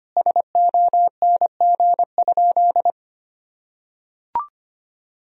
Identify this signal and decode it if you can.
Morse code, heard as a keyed tone, about 25 wpm, 700 Hz. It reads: SONG?
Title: Sentences from Top 1000 Words - Review Part 2 - 25wpm